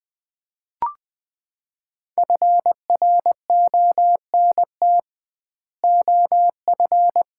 Morse code, heard as a keyed tone, about 20 wpm, 700 Hz. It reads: FRONT OF
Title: 2x Word-Spacing - Sets of 2 Words - 20wpm